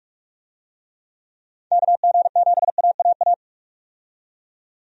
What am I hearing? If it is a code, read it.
Morse code, heard as a keyed tone, about 43 wpm, 700 Hz. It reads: KG7AAA
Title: USA Call Signs - 6 Characters - 45wpm